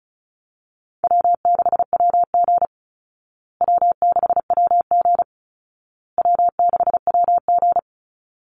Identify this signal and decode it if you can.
Morse code, heard as a keyed tone, about 34 wpm, 700 Hz. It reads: W6WG W6WG W6WG
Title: Call Signs Repeated 2x - 35wpm